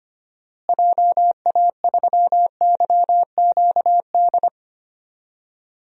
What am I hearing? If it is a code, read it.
Morse code, heard as a keyed tone, about 25 wpm, 700 Hz. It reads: JA3YQD
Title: DXCC-100 Call Signs - 6 Characters - Repeated 3x - 25wpm